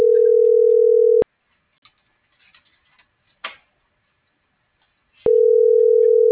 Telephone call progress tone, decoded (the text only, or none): ringback tone